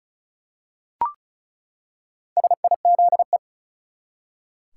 Morse code, heard as a keyed tone, about 36 wpm, 700 Hz. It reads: SIZE
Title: Four Letter Words - 35wpm